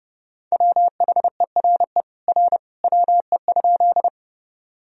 Morse code, heard as a keyed tone, about 30 wpm, 700 Hz. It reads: WHERE R WE?
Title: Sentences from Top 200 Words and Abbreviations 30wpm